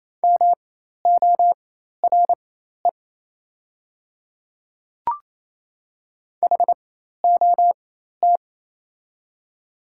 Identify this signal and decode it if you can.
Morse code, heard as a keyed tone, about 27 wpm, spaced as for 13 wpm, 700 Hz. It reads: MORE HOT
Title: 4x Character Spacing - Top 500 Words - 28wpm